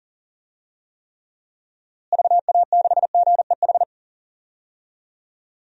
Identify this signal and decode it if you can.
Morse code, heard as a keyed tone, about 40 wpm, 700 Hz. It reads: VA6GEH
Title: DXCC-100 Call Signs - 6 Characters - Repeated 3x - 40wpm